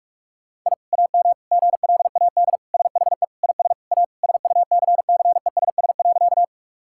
Morse code, heard as a keyed tone, about 41 wpm, 700 Hz. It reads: I AM GLAD SHE IS A SUCCESS.
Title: Sentences from Top 1000 Words - Review Part 6 - 45wpm